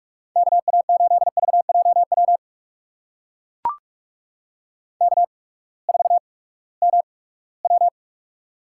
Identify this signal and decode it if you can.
Morse code, heard as a keyed tone, about 44 wpm, spaced as for 22 wpm, 700 Hz. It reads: KA8VJW K4MW
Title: USA Call Signs - 8x Character Spacing - 45wpm